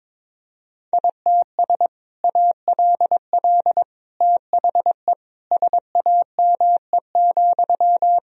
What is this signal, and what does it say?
Morse code, heard as a keyed tone, about 22 wpm, 700 Hz. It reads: ITS ALL THE SAME,
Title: Sentences from Top 1000 Words - Review Part 3 - 22wpm